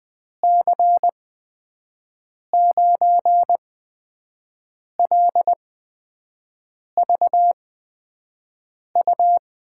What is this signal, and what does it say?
Morse code, heard as a keyed tone, about 20 wpm, spaced as for 5 wpm, 700 Hz. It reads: C9LVU
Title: USA Call Signs - 6 Characters - 8x Character Spacing - 20wpm